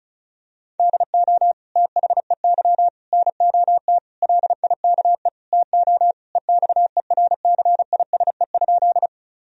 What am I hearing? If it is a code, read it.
Morse code, heard as a keyed tone, about 36 wpm, 700 Hz. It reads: DO THEY NOT LIKE TO EXERCISE?
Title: Sentences from Top 1000 Words - Review Part 4 - 35wpm